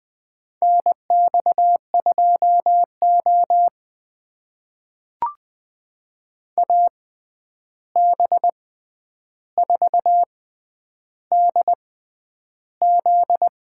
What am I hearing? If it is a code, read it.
Morse code, heard as a keyed tone, about 21 wpm, spaced as for 10 wpm, 700 Hz. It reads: NX2O AB4DZ
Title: USA Call Signs - 6x Character Spacing - 20wpm